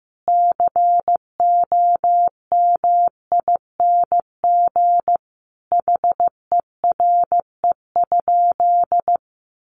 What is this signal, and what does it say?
Morse code, heard as a keyed tone, about 14 wpm, 700 Hz. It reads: COMING HERE?
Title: Sentences from Top 1300 Words 15wpm